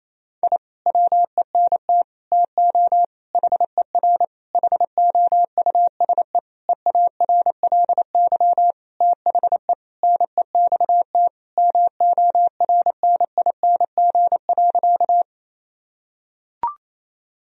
Morse code, heard as a keyed tone, about 28 wpm, 700 Hz. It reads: I WENT TO HER HOUSE EARLY THE NEXT MORNING.
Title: Sentences from Top 1000 Words - Review Part 2 - 28wpm